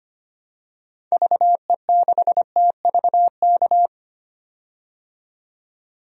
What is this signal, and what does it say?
Morse code, heard as a keyed tone, about 25 wpm, 700 Hz. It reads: VE6TVK
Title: DXCC-100 Call Signs - 6 Characters - 25wpm